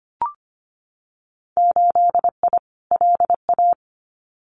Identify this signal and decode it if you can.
Morse code, heard as a keyed tone, about 25 wpm, 700 Hz. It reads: 8I LA
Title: Winter Field Day Exchanges Encore 25wpm